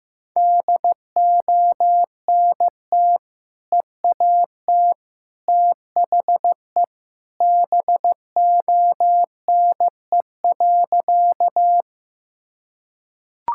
Morse code, heard as a keyed tone, about 15 wpm, 700 Hz. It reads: DONT EAT THE BONE.
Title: Sentences from Top 1600 Words 15wpm